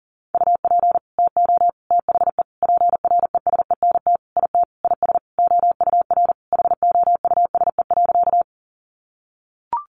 Morse code, heard as a keyed tone, about 40 wpm, 700 Hz. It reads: UP TO THE PRESENT IT IS OUR HOUSE.
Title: Sentences from Top 800 Words 40wpm